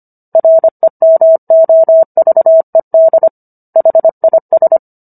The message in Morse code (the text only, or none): REMOVED HIS